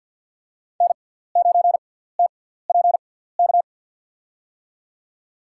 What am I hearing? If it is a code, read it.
Morse code, heard as a keyed tone, about 50 wpm, spaced as for 17 wpm, 700 Hz. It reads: N9TPK
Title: USA Call Signs - 5 Characters - 6x Character Spacing - 50wpm